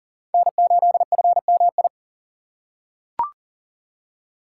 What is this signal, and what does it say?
Morse code, heard as a keyed tone, about 37 wpm, 700 Hz. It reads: N8FMI
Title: USA Call Signs - 5 Characters - 4x Character Spacing - 40wpm